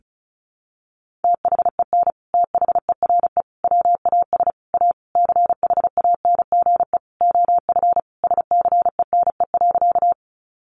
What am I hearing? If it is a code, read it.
Morse code, heard as a keyed tone, about 34 wpm, 700 Hz. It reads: THEN THERE WAS A CHANGE OF SCENE.
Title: Sentences from Top 1100 Words 35wpm